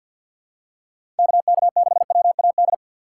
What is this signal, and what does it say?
Morse code, heard as a keyed tone, about 49 wpm, 700 Hz. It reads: KK6WAD
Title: USA Call Signs - 6 Characters - 3x Character Spacing - 50wpm